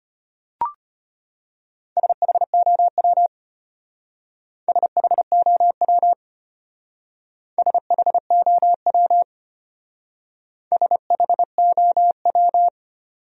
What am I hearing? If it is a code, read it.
Morse code, heard as a keyed tone, about 31 wpm, 700 Hz. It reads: SHOW SHOW SHOW SHOW
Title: Speed-Racing Top 500 Words 25wpm